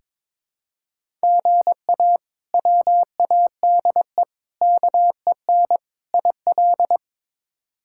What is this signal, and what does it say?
Morse code, heard as a keyed tone, about 22 wpm, 700 Hz. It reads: GA WADE KEN IL
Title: K1USN SST QSO Practice - 22wpm